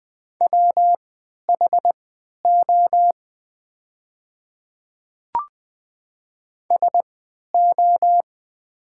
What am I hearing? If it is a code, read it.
Morse code, heard as a keyed tone, about 20 wpm, spaced as for 11 wpm, 700 Hz. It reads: WHO SO
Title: 3x Character Spacing - Top 100 Words - 20wpm